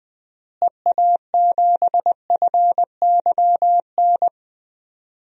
Morse code, heard as a keyed tone, about 20 wpm, 700 Hz. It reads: EA7FYN
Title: DXCC-100 Call Signs - 6 Characters - Repeated 2x - 20wpm